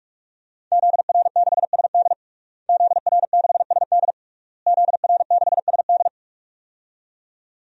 Morse code, heard as a keyed tone, about 46 wpm, 700 Hz. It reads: ZR6SD ZR6SD ZR6SD
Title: DXCC-100 Call Signs - 5 Characters - Repeated 3x - 45wpm